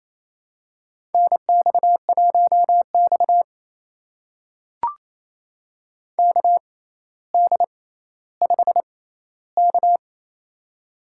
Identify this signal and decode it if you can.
Morse code, heard as a keyed tone, about 28 wpm, spaced as for 14 wpm, 700 Hz. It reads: NX1X KD5K
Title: USA Call Signs - 4 Characters - 6x Character Spacing - 28wpm